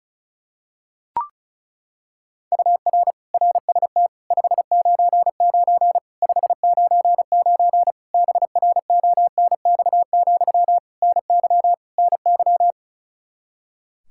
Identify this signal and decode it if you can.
Morse code, heard as a keyed tone, about 36 wpm, 700 Hz. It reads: UR RST 599 599 BRONX, NY NY
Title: Practice QSO - Exchange 1 - Beginning - 35wpm